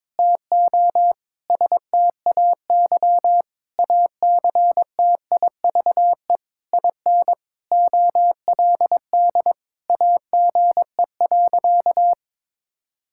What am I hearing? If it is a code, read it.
Morse code, heard as a keyed tone, about 22 wpm, 700 Hz. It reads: TO STAY ACTIVE IN OLD AGE.
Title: Sentences from Top 1000 Words - Review Part 6 - 22wpm